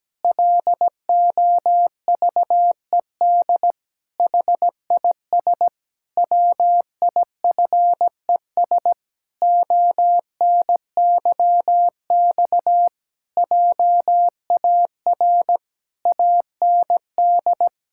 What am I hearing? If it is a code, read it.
Morse code, heard as a keyed tone, about 17 wpm, 700 Hz. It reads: LOVED HIS WIFES ONYX JAR AND
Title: Pangram Challenge v2 - 17wpm